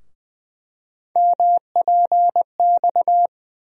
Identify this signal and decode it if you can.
Morse code, heard as a keyed tone, about 20 wpm, 700 Hz. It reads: MPX
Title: USA Call Sign Postfixes - 3 Letters 20wpm